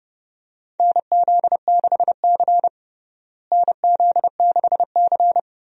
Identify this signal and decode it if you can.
Morse code, heard as a keyed tone, about 30 wpm, 700 Hz. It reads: NZ6C NZ6C
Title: USA Call Signs - 4 Characters - Repeated 2x - 30wpm